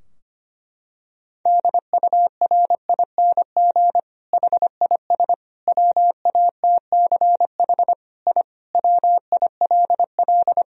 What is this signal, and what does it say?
Morse code, heard as a keyed tone, about 25 wpm, 700 Hz. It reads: DURING HIS WATCH I WILL
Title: Sentences from Top 1000 Words - Review Part 2 - 25wpm